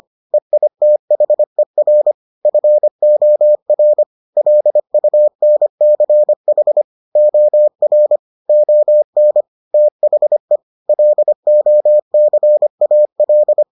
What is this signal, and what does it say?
Morse code, heard as a keyed tone, about 25 wpm, 600 Hz. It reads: EITHER FOR LUNCH OR ON THE LOCAL